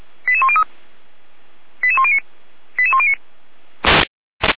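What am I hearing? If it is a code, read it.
Five-tone selective call (ZVEI), heard as quick runs of five tones, tones of about 70 ms. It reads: 88103 88109 88109
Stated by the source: ZVEI five-tone selective call 88102 received with an SDR, Feuerwehr Orlamünde.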